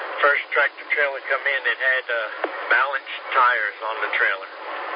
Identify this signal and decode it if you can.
Emergency Alert System (SAME) tones, heard as harsh buzzing bursts, with speech. It is heard from 1.2 to 4.4 s.